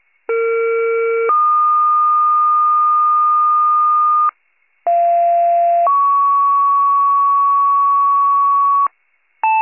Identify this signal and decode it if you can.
Two-tone page, heard as a one-second tone, then a three-second tone, about 460 then 1220 Hz, about 690 then 1120 Hz. It is heard from 0.3 to 8.9 s.